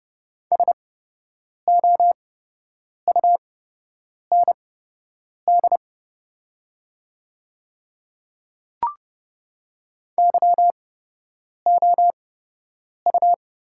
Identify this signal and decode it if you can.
Morse code, heard as a keyed tone, about 31 wpm, spaced as for 8 wpm, 700 Hz. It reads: SOUND YOU
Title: 8x Character Spacing - Top 100 Words - 30wpm